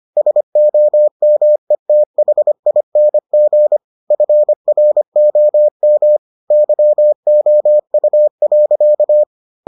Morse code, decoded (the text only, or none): SOMETHING FROM YOU.